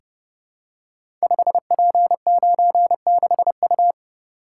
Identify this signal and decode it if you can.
Morse code, heard as a keyed tone, about 30 wpm, 700 Hz. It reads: 5P96U
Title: Code Groups - 5 Letter:Numbers - 30wpm